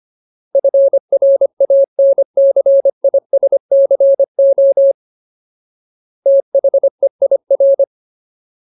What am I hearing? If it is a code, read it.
Morse code, heard as a keyed tone, about 25 wpm, 550 Hz. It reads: FRANCISCO THEIR
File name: Ultimate Word List / 25WPM Ultimate